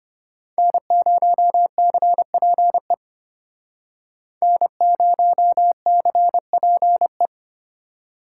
Morse code, heard as a keyed tone, about 33 wpm, 700 Hz. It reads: N0CPE N0CPE